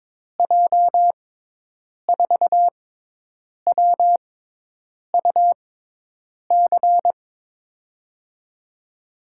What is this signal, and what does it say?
Morse code, heard as a keyed tone, about 22 wpm, spaced as for 8 wpm, 700 Hz. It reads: J4WUC
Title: USA Call Signs - 6 Characters - 6x Character Spacing - 22wpm